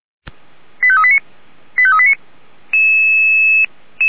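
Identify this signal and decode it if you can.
Five-tone selective call (ZVEI), heard as quick runs of five tones, tones of about 70 ms. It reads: 85389 85389
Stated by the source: ZVEI five-tone selective call 85749, received with an SDR, archived by Feuerwehr Orlamünde.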